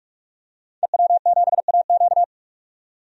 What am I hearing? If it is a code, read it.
Morse code, heard as a keyed tone, about 45 wpm, 700 Hz. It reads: EW7AQ
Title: DXCC-100 Call Signs - 5 Characters - 45wpm